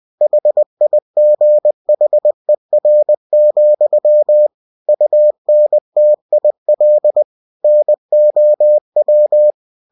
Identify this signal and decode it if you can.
Morse code, heard as a keyed tone, about 20 wpm, 600 Hz. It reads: HIGHER, UNTIL NOW
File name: cw-books / winnie-the-pooh / winnie-20wpm-page-0091